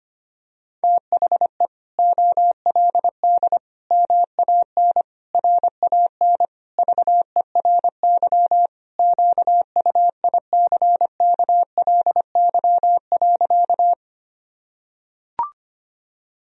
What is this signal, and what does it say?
Morse code, heard as a keyed tone, about 25 wpm, 700 Hz. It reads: THE OLD MAN RAN VERY QUICKLY.
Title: Sentences from Top 1000 Words - Review Part 3 - 25wpm